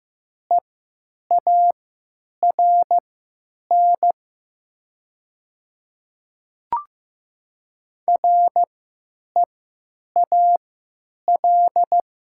Morse code, heard as a keyed tone, about 15 wpm, spaced as for 9 wpm, 700 Hz. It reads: EARN REAL
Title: 3x Character Spacing - Top 300 Words - 15wpm